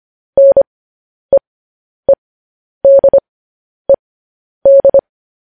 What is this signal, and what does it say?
Morse code, heard as a keyed tone, about 25 wpm, spaced as for 10 wpm, 550 Hz. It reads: NEEDED
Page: news-headlines-in-morse-code-at-10-wpm